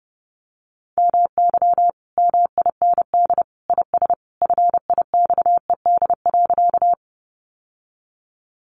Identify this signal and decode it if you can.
Morse code, heard as a keyed tone, about 30 wpm, 700 Hz. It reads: MY MIND IS FIXED.
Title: Sentences from Top 1100 Words 30wpm